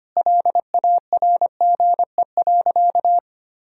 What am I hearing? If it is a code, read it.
Morse code, heard as a keyed tone, about 25 wpm, 700 Hz. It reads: LARGE.
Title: Sentences from Top 1000 Words - Review Part 1 - 25wpm